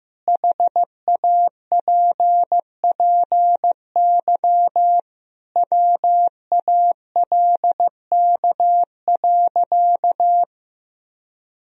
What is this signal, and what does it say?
Morse code, heard as a keyed tone, about 15 wpm, 700 Hz. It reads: HAPPY WALK.
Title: Sentences from Top 1000 Words - Review Part 1 - 15wpm